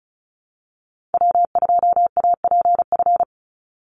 Morse code, heard as a keyed tone, about 34 wpm, 700 Hz. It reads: W2APF
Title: Call Signs Repeated 2x - 35wpm